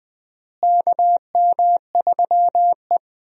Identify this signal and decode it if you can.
Morse code, heard as a keyed tone, about 20 wpm, 700 Hz. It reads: KM3E
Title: USA Call Signs - 4 Characters - Repeated 3x - 20wpm